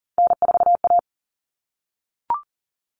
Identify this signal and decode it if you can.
Morse code, heard as a keyed tone, about 40 wpm, 700 Hz. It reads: N4A
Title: All Cut Numbers - 3-Digit 40wpm